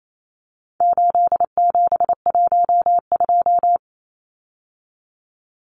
Morse code, heard as a keyed tone, about 28 wpm, 700 Hz. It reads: 8712
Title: Numbers - 4-Digit 28wpm